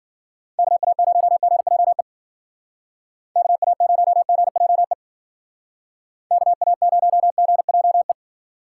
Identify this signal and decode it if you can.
Morse code, heard as a keyed tone, about 52 wpm, 700 Hz. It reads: KA0GJE KA0GJE KA0GJE